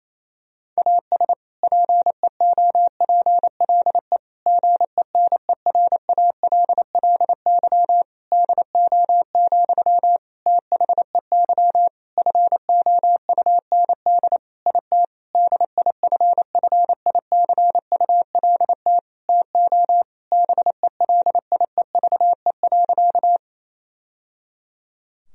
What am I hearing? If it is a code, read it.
Morse code, heard as a keyed tone, about 28 wpm, 700 Hz. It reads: AS PEOPLE GENERALLY DO, THEY FOUND IT DIFFICULT TO BELIEVE.
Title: Sentences from Top 1000 Words - Review Part 4 - 28wpm